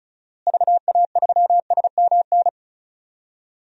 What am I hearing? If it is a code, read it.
Morse code, heard as a keyed tone, about 36 wpm, 700 Hz. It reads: VA3SMN
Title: DXCC-100 Call Signs - 6 Characters - Repeated 2x - 35wpm